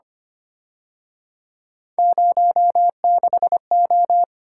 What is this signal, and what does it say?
Morse code, heard as a keyed tone, about 29 wpm, 700 Hz. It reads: 06O
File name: Speed-Racing - Three Letter-Number Combinations - 25wpm